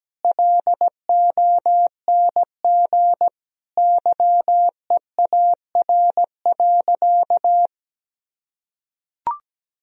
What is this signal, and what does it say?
Morse code, heard as a keyed tone, about 17 wpm, 700 Hz. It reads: LONG YEAR.
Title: Sentences from Top 1000 Words - Review Part 1 - 17wpm